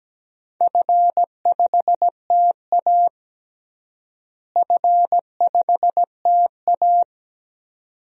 Morse code, heard as a keyed tone, about 17 wpm, 700 Hz. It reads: F5TA F5TA
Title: DXCC-100 Call Signs - 4 Characters - Repeated 2x - 17wpm